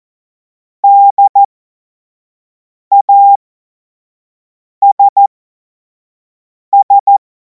Morse code, heard as a keyed tone, about 14 wpm, spaced as for 5 wpm, 800 Hz. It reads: DASS
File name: Morsekurs / Witze / Witz 09. Juni 2026 25 F70